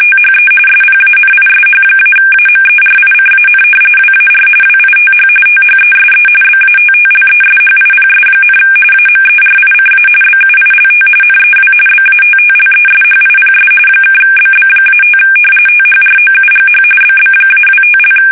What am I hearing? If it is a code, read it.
RTTY, about 45.45 baud. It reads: THE SPANISH-AMERICAN WAR WITH THESE BATTLESHIPS, AMERICA BUILT NEW BATTLESHIPS (BB-5 THROUGH BB-25)
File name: fsk850 / battleshipnj